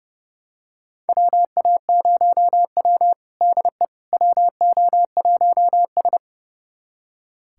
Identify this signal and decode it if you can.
Morse code, heard as a keyed tone, about 30 wpm, 700 Hz. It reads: WA0W DE WO1S
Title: USA Callsign DE Callsign - 4 Characters - 30wpm